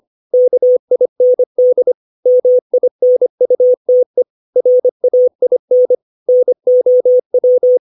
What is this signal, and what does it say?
Morse code, heard as a keyed tone, about 25 wpm, 500 Hz. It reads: KIND MINUTE RAIN NOW
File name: 632 words / 632 Words 25WPM